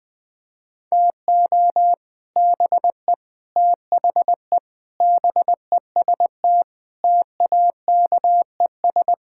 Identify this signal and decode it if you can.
Morse code, heard as a keyed tone, about 21 wpm, 700 Hz. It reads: TO BE THE BEST TAKES
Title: Sentences with 3-Word Phrases - 20wpm